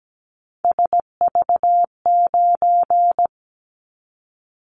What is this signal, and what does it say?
Morse code, heard as a keyed tone, about 17 wpm, 700 Hz. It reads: SV9
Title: Callsign Prefixes for Top 100 DXCC - 17wpm